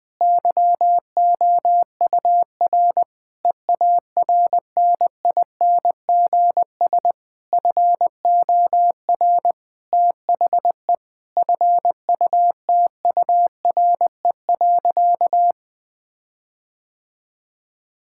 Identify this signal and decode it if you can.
Morse code, heard as a keyed tone, about 20 wpm, 700 Hz. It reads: YOUR EARNINGS FOR THE FUTURE.
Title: Difficult Sentences with Hints - 20wpm